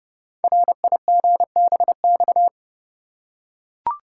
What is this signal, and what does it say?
Morse code, heard as a keyed tone, about 29 wpm, 700 Hz. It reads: RIGBX
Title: Code Groups - 5 Letters - 30wpm